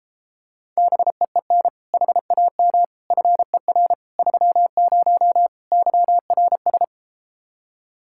Morse code, heard as a keyed tone, about 34 wpm, 700 Hz. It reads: BEEN HAM FER 30 YRS